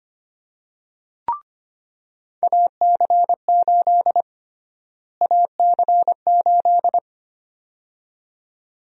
Morse code, heard as a keyed tone, about 25 wpm, 700 Hz. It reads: AC8 AC8
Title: USA Call Sign Prefixes - Repeated 2x - 25wpm